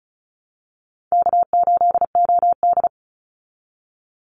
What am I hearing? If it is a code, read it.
Morse code, heard as a keyed tone, about 34 wpm, 700 Hz. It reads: K8OD
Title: Call Signs Repeated 3x - 35wpm